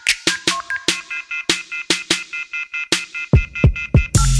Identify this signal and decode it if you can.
Off-hook warning tone, 1400 + 2060 + 2450 + 2600 Hz.